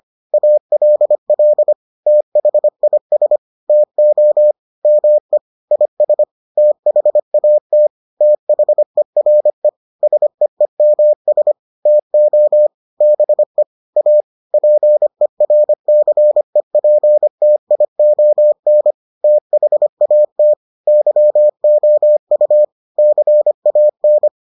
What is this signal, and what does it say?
Morse code, heard as a keyed tone, about 25 wpm, 600 Hz. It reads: ALL THIS TO ME IS THAT THERE SEEMS TO BE A PERCEPTION THAT YOU CAN